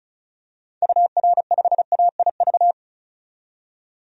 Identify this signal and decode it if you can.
Morse code, heard as a keyed tone, about 36 wpm, 700 Hz. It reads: UR5AIV
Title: DXCC-100 Call Signs - 6 Characters - 35wpm